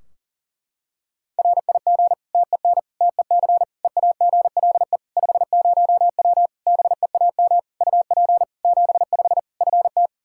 Morse code, heard as a keyed tone, about 37 wpm, 700 Hz. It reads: RIG TEN TEC EAGLE 50W BEAM UP 75 FT
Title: Practice QSO - Exchange 2 - Info - 40wpm